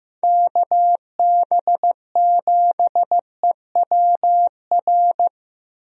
Morse code, heard as a keyed tone, about 15 wpm, 700 Hz. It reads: KB7EWR
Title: USA Call Signs - 6 Characters - Repeated 2x - 15wpm